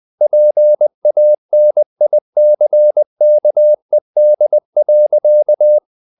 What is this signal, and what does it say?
Morse code, heard as a keyed tone, about 20 wpm, 600 Hz. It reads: PANICKED.
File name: the old man and the sea / the old man and the sea part 1 20wpm